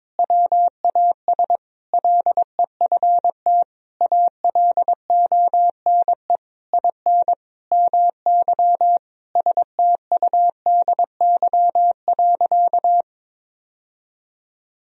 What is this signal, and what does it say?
Morse code, heard as a keyed tone, about 22 wpm, 700 Hz. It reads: WAS LEFT ALONE IN MY STUDY.
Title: Sentences from Top 1000 Words - Review Part 1 - 22wpm